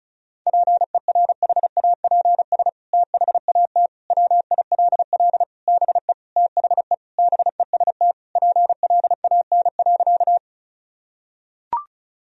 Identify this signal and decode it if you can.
Morse code, heard as a keyed tone, about 36 wpm, 700 Hz. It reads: PERHAPS THAT WILL BE THE BEST PLAN.
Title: Sentences from Top 1000 Words - Review Part 2 - 35wpm